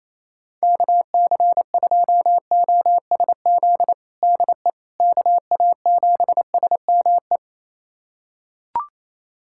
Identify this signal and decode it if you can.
Morse code, heard as a keyed tone, about 28 wpm, 700 Hz. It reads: KC2OSZ DE KA7SME
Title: USA Callsign DE Callsign - 6 Characters - 28wpm